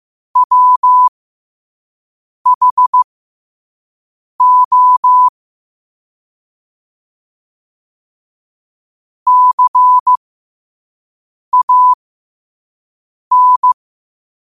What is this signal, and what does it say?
Morse code, heard as a keyed tone, about 15 wpm, spaced as for 5 wpm, 1000 Hz. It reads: WHO CAN
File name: qotdfiles / 5wpm